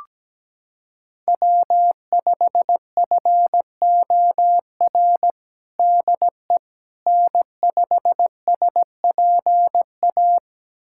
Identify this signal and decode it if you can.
Morse code, heard as a keyed tone, about 17 wpm, 700 Hz. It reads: W5FOR DE N5SPA
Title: USA Callsign DE Callsign - 5 Characters - 17wpm